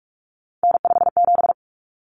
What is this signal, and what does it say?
Morse code, heard as a keyed tone, about 44 wpm, 700 Hz. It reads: N57
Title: Common Cut Numbers - 3-Digit 45wpm